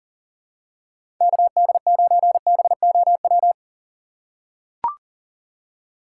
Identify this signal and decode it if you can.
Morse code, heard as a keyed tone, about 40 wpm, 700 Hz. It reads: KD9BOW
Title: USA Call Signs - 2x Character Spacing - 40wpm